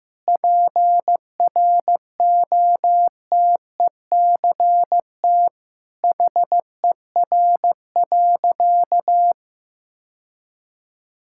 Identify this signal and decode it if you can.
Morse code, heard as a keyed tone, about 16 wpm, 700 Hz. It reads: PROTECT HER.
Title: Sentences from Top 1600 Words 15wpm